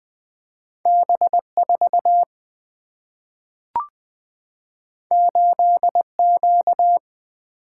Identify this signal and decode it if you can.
Morse code, heard as a keyed tone, about 20 wpm, 700 Hz. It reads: B 4 8 Q
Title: Single-Letter Single-Number 20wpm